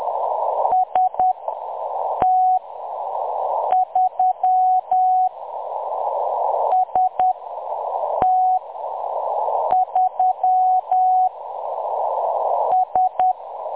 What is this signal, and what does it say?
Morse code, heard as a keyed tone, about 10 wpm, spaced as for 6 wpm, 740 Hz. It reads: ST3 ST3 S
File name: M23-ST3- -0858utc-10184khz